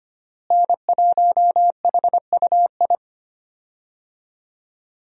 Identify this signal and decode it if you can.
Morse code, heard as a keyed tone, about 25 wpm, 700 Hz. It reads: N1HUI
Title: USA Call Signs - 5 Characters - Repeated 3x - 25wpm